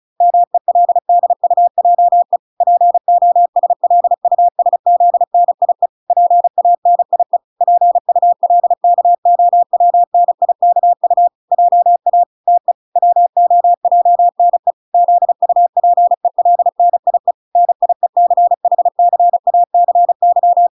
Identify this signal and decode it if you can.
Morse code, heard as a keyed tone, about 34 wpm, 700 Hz. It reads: MELDUJE POSLUSZNIE PANIE PULKOWNIKU JA TE WOJNE ZUPELNIE NIECHCACY